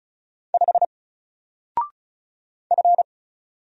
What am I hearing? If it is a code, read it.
Morse code, heard as a keyed tone, about 35 wpm, 700 Hz. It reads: 5 F
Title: Single Letter-Number 35wpm